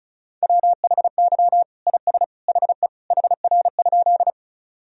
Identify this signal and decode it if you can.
Morse code, heard as a keyed tone, about 36 wpm, 700 Hz. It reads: WHY IS HE HR?
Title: Sentences from Top 200 Words and Abbreviations 35wpm